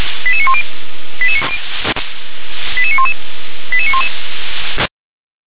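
Five-tone selective call (ZVEI), heard as quick runs of five tones, tones of about 70 ms. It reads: 88910 88911 88911
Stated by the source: ZVEI five-tone selective call 88148 received on SDR.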